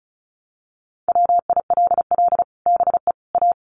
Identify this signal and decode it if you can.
Morse code, heard as a keyed tone, about 34 wpm, 700 Hz. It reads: WILL BE A
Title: Sets of 3 Words 35wpm